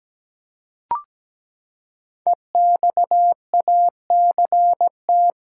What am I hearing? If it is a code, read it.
Morse code, heard as a keyed tone, about 17 wpm, 700 Hz. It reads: EXACT
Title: Five Letter Words - 17wpm